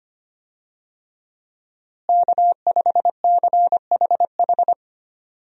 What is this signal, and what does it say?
Morse code, heard as a keyed tone, about 25 wpm, 700 Hz. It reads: K5CHH
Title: USA Call Signs - 5 Characters - Repeated 3x - 25wpm